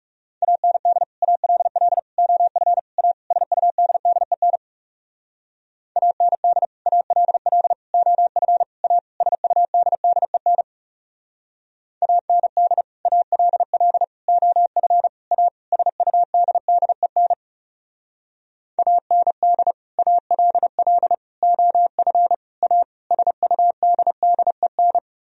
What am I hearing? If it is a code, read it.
Morse code, heard as a keyed tone, about 37 wpm, 700 Hz. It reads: AND ALL OF A SUDDEN AND ALL OF A SUDDEN AND ALL OF A SUDDEN AND ALL OF A SUDDEN
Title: Speed-Racing - Sets of 5 Words 30wpm